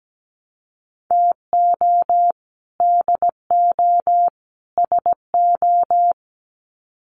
Morse code, heard as a keyed tone, about 17 wpm, 700 Hz. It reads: TO DO SO
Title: Sets of 3 Words 17wpm